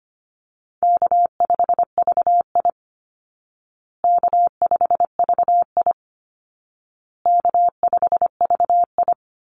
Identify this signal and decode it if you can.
Morse code, heard as a keyed tone, about 25 wpm, 700 Hz. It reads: K5VI K5VI K5VI
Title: Easy Call Signs Repeated 3x - 25wpm